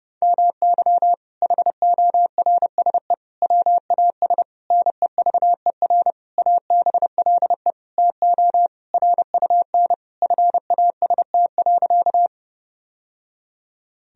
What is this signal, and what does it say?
Morse code, heard as a keyed tone, about 30 wpm, 700 Hz. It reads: MY HORSE WAS NEVER ABLE TO RUN FAST.
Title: Sentences from Top 1000 Words - Review Part 2 - 30wpm